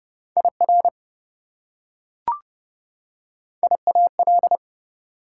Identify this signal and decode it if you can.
Morse code, heard as a keyed tone, about 30 wpm, 700 Hz. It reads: IR IAL
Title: Prefixes and Suffixes 30wpm